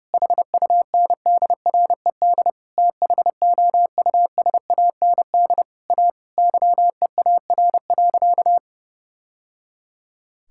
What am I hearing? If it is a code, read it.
Morse code, heard as a keyed tone, about 30 wpm, 700 Hz. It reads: HUNDRED THOUSAND A YEAR.
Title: Sentences from Top 1000 Words - Review Part 5 - 30wpm